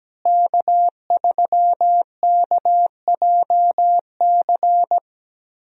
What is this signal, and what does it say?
Morse code, heard as a keyed tone, about 17 wpm, 700 Hz. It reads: K3KJC
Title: USA Call Signs - 5 Characters - Repeated 2x - 17wpm